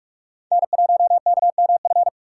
Morse code, heard as a keyed tone, about 41 wpm, 700 Hz. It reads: N1KMF
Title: USA Call Signs - 2x Character Spacing - 45wpm